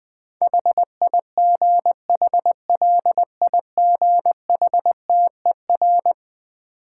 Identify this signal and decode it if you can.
Morse code, heard as a keyed tone, about 21 wpm, 700 Hz. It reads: HIGHLIGHTER
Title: Difficult Words with Hints - 20wpm